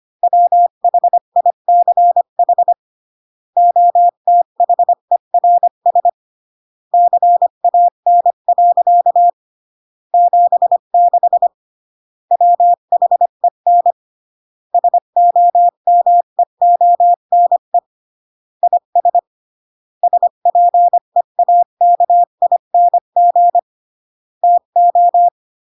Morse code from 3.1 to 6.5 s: OTHERS